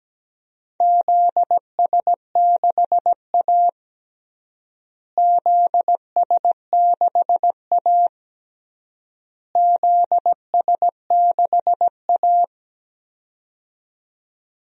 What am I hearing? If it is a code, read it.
Morse code, heard as a keyed tone, about 17 wpm, 700 Hz. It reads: ZS6A ZS6A ZS6A
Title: DXCC-100 Call Signs - 4 Characters - Repeated 3x - 17wpm